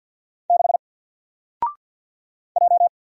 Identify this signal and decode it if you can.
Morse code, heard as a keyed tone, about 53 wpm, 700 Hz. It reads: 6 J
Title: Single Letter-Number 50wpm